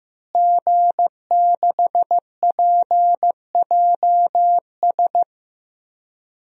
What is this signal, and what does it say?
Morse code, heard as a keyed tone, about 16 wpm, 700 Hz. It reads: G6PJS
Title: DXCC-100 Call Signs - 5 Characters - Repeated 2x - 15wpm